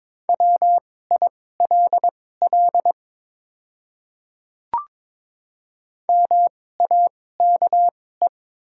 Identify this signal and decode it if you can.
Morse code, heard as a keyed tone, about 22 wpm, spaced as for 16 wpm, 700 Hz. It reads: WILL MAKE
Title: 2x Character Spacing - Top 200 Words - 22wpm